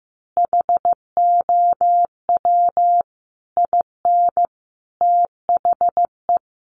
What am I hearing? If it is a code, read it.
Morse code, heard as a keyed tone, about 15 wpm, 700 Hz. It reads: HOW IN THE
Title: Sentences from Top 200 Words 15wpm